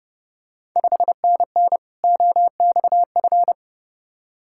Morse code, heard as a keyed tone, about 32 wpm, 700 Hz. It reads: 5NN OXF
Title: Ontario QSO Party Exchange 30wpm